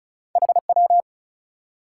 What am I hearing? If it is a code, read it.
Morse code, heard as a keyed tone, about 36 wpm, 700 Hz. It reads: HW
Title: Top 180 QSO Elements 35wpm